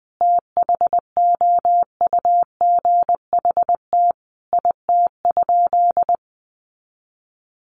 Morse code, heard as a keyed tone, about 20 wpm, 700 Hz. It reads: THOUGHT IT?
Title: Sentences from Top 300 Words 20wpm